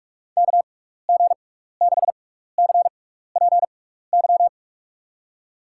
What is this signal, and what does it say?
Morse code, heard as a keyed tone, about 46 wpm, spaced as for 16 wpm, 700 Hz. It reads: KG6CPY
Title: USA Call Signs - 6 Characters - 6x Character Spacing - 45wpm